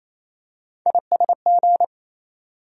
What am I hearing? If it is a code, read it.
Morse code, heard as a keyed tone, about 28 wpm, 700 Hz. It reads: ISG
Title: USA Call Sign Postfixes - 3 Letters 28wpm